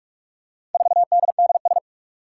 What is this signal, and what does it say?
Morse code, heard as a keyed tone, about 44 wpm, 700 Hz. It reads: 4DDS
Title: Four Challenging Letter-Number Combinations 45wpm